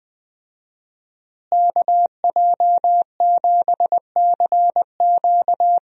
Morse code, heard as a keyed tone, about 20 wpm, 700 Hz. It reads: KJ7CQ
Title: USA Call Signs - 5 Characters - Repeated 3x - 20wpm